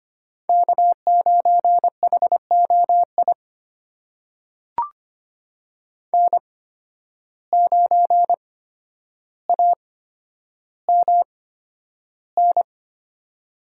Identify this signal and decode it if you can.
Morse code, heard as a keyed tone, about 25 wpm, spaced as for 11 wpm, 700 Hz. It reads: K9HOI N9AMN
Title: USA Call Signs - 8x Character Spacing - 25wpm